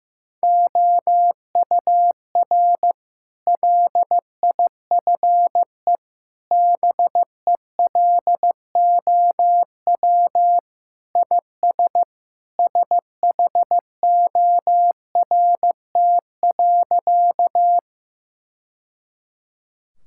Morse code, heard as a keyed tone, about 16 wpm, 700 Hz. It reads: OUR LIFE BELOW IS SHORT.
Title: Sentences from Top 1000 Words - Review Part 5 - 15wpm